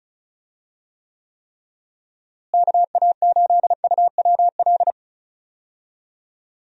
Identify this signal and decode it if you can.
Morse code, heard as a keyed tone, about 36 wpm, 700 Hz. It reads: KA8UWL
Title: USA Call Signs - 6 Characters - Repeated 3x - 35wpm